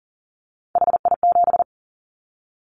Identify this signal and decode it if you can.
Morse code, heard as a keyed tone, about 40 wpm, 700 Hz. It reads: HI7
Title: Callsign Prefixes for Top 100 DXCC - 40wpm